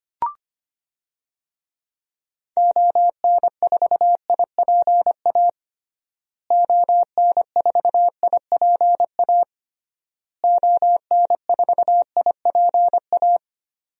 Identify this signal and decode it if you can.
Morse code, heard as a keyed tone, about 25 wpm, 700 Hz. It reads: ON4IPA ON4IPA ON4IPA
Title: DXCC-100 Call Signs - 6 Characters - Repeated 3x - 25wpm